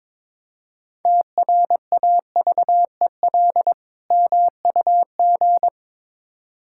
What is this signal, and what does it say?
Morse code, heard as a keyed tone, about 22 wpm, 700 Hz. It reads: TRAVEL MUG
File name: Difficult Words with Hints - 22wpm